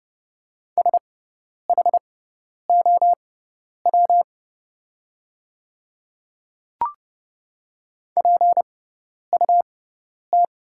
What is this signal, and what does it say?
Morse code, heard as a keyed tone, about 30 wpm, spaced as for 10 wpm, 700 Hz. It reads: SHOW PUT
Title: 6x Character Spacing - Top 200 Words - 30wpm